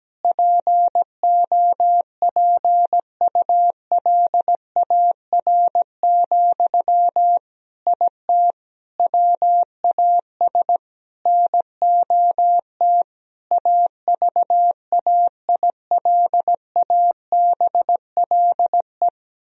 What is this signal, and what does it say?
Morse code, heard as a keyed tone, about 17 wpm, 700 Hz. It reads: POPULAR, IT WAS NOT AVAILABLE
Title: Sentences from Top 1000 Words - Review Part 3 - 17wpm